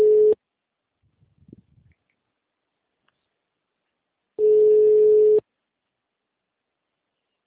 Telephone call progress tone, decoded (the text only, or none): ringback tone